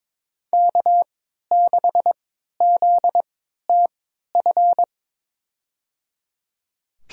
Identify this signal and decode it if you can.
Morse code, heard as a keyed tone, about 22 wpm, spaced as for 12 wpm, 700 Hz. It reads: K6ZTF
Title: USA Call Signs - 5 Characters - 3x Character Spacing - 22wpm